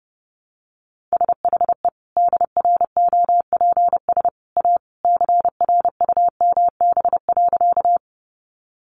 Morse code, heard as a keyed tone, about 30 wpm, 700 Hz. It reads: SHE DROPS A CRUMB.
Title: Easy Phrases 30wpm